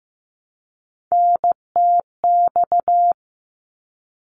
Morse code, heard as a keyed tone, about 15 wpm, 700 Hz. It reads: NTX
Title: ARRL Sections 15wpm